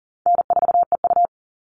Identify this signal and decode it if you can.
Morse code, heard as a keyed tone, about 39 wpm, 700 Hz. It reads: N4EU
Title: All Cut Numbers - 4-Digit 40wpm